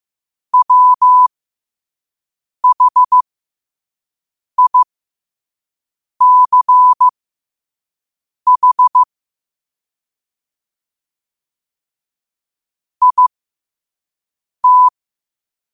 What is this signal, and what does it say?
Morse code, heard as a keyed tone, about 15 wpm, spaced as for 5 wpm, 1000 Hz. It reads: WHICH IT
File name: qotdfiles / 5wpm